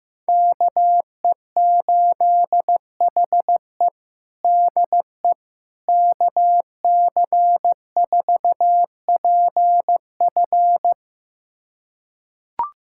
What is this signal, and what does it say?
Morse code, heard as a keyed tone, about 16 wpm, 700 Hz. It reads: KE8HE DE KC4PF
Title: USA Callsign DE Callsign - 5 Characters - 15wpm